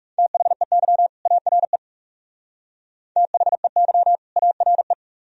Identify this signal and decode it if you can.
Morse code, heard as a keyed tone, about 35 wpm, 700 Hz. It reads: THEY ARE THEY ARE
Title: Speed-Racing - Sets of 2 Words 30wpm